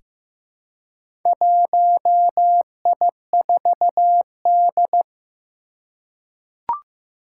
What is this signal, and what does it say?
Morse code, heard as a keyed tone, about 15 wpm, 700 Hz. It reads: 1I4D
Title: Four Challenging Letter-Number Combinations 15wpm